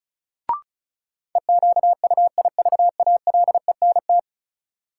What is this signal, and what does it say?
Morse code, heard as a keyed tone, about 36 wpm, 700 Hz. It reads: EQUIVALENT
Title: Sentences from Top 1600 Words - New Word Drill Alt - 35wpm